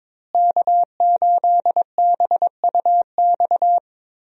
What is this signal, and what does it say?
Morse code, heard as a keyed tone, about 22 wpm, 700 Hz. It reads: K8BUX
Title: USA Call Signs - 5 Characters - Repeated 3x - 22wpm